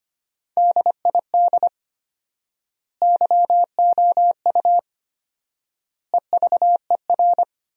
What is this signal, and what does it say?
Morse code, heard as a keyed tone, about 25 wpm, 700 Hz. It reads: DID YOU EVER
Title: 4x Word-Spacing - Sentences from Top 400 Words - 25wpm